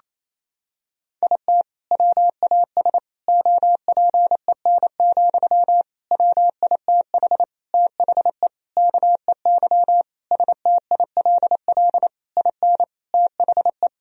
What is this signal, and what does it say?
Morse code, heard as a keyed tone, about 27 wpm, 700 Hz. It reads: IT WAS OPEN, WITH THE KEY STILL IN THE
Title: Sentences from Top 1000 Words - Review Part 4 - 28wpm